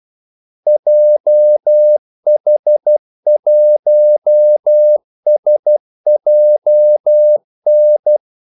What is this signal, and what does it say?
Morse code, heard as a keyed tone, about 12 wpm, 600 Hz. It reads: JH1SJN
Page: 2 ekstra ordmellemrum mellem kaldesignalerne